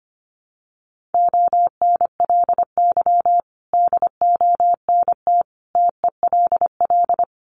Morse code, heard as a keyed tone, about 25 wpm, 700 Hz. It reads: ONLY DONT TELL
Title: Sentences from Top 900 Words 25wpm